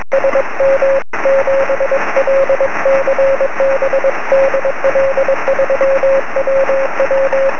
Morse code, heard as a keyed tone, about 21 wpm, 540 Hz, with radio noise. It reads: SM7LCB DL3WW
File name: sm7lcb / 20 23cm / 2129 dl3ww 23cm